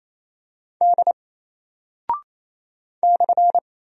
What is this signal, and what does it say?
Morse code, heard as a keyed tone, about 27 wpm, 700 Hz. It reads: D /
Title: Single Character (Letter:Number:Punctuation:Prosign) 28wpm